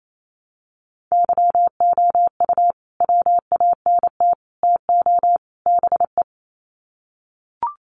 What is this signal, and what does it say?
Morse code, heard as a keyed tone, about 28 wpm, 700 Hz. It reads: YOU WANT TO BE
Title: Sets of 4 Words 28wpm